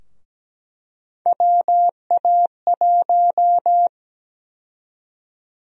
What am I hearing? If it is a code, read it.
Morse code, heard as a keyed tone, about 17 wpm, 700 Hz. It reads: WA1
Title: USA Call Sign Prefixes - 17wpm